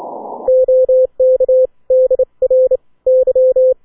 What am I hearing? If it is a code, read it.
Morse code, heard as a keyed tone, about 29 wpm, 520 Hz. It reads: OK DR Y